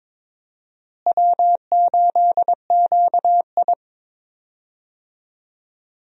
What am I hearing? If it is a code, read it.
Morse code, heard as a keyed tone, about 22 wpm, 700 Hz. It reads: W8QI
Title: USA Call Signs - 4 Characters - Repeated 3x - 22wpm